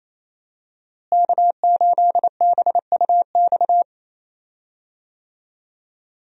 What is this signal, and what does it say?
Morse code, heard as a keyed tone, about 28 wpm, 700 Hz. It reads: K8BUX
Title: USA Call Signs - 5 Characters - Repeated 3x - 28wpm